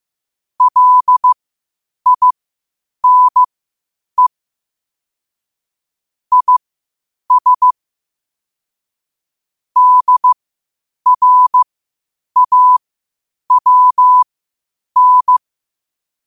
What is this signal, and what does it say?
Morse code, heard as a keyed tone, about 15 wpm, spaced as for 8 wpm, 1000 Hz. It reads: LINE IS DRAWN
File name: qotdfiles / 8wpm